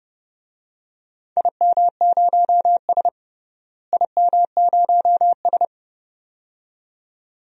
Morse code, heard as a keyed tone, about 32 wpm, 700 Hz. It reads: IM0S IM0S
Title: DXCC-100 Call Signs - 4 Characters - Repeated 2x - 30wpm